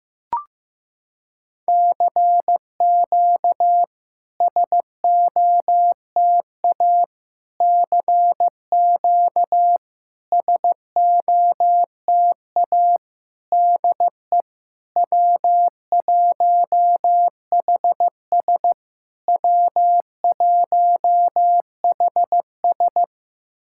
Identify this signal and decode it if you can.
Morse code, heard as a keyed tone, about 14 wpm, 690 Hz. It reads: CQ SOTA CQ SOTA DE W1HS W1HS
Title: SOTA QSO Practice 15wpm